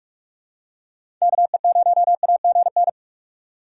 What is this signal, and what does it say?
Morse code, heard as a keyed tone, about 46 wpm, 690 Hz. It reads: KE0AGN
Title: USA Call Signs - 4x Character Spacing - 45wpm